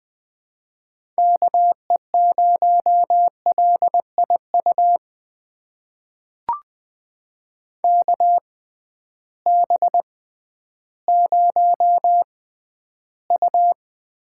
Morse code, heard as a keyed tone, about 20 wpm, spaced as for 12 wpm, 700 Hz. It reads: KE0LIU KB0U
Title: USA Call Signs - 6 Characters - 6x Character Spacing - 20wpm